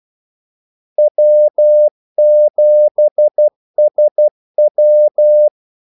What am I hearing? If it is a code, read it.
Morse code, heard as a keyed tone, about 12 wpm, 600 Hz. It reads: W7SW
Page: 2 ekstra ordmellemrum mellem kaldesignalerne